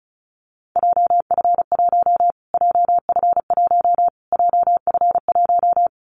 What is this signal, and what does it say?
Morse code, heard as a keyed tone, about 34 wpm, 700 Hz. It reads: JF1 JF1 JF1
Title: Callsign Prefixes for Top 25 DXCC - Repeated 3x - 35wpm